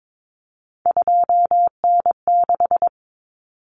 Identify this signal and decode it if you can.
Morse code, heard as a keyed tone, about 22 wpm, 700 Hz. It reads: 2N6
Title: Common Cut Numbers - 3-Digit 22wpm